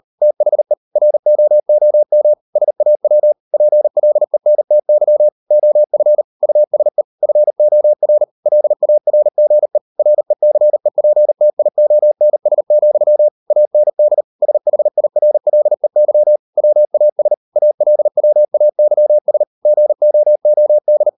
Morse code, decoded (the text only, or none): THE ROOM SAW PLENTY OF USE FOR LARGE RECEPTIONS, AND SHIRLEY WAS ALWAYS GOOD